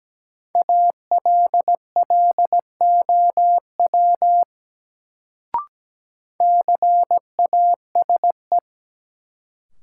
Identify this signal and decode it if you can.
Morse code, heard as a keyed tone, about 17 wpm, 700 Hz. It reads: ALLOW CASE
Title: Sentences from Top 700 Words - New Word Drill - 17wpm